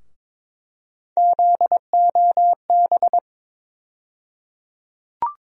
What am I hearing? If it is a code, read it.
Morse code, heard as a keyed tone, about 22 wpm, 700 Hz. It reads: ZOB
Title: USA Call Sign Postfixes - 3 Letters 22wpm